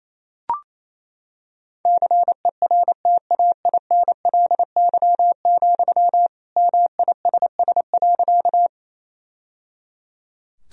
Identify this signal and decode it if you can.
Morse code, heard as a keyed tone, about 28 wpm, 700 Hz. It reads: CERTAINLY, MISS.
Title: Sentences from Top 1000 Words - Review Part 3 - 28wpm